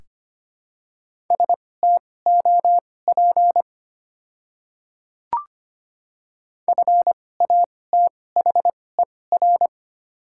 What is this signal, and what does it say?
Morse code, heard as a keyed tone, about 25 wpm, spaced as for 18 wpm, 700 Hz. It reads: STOP FATHER
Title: 2x Character Spacing - Top 300 Words - 25wpm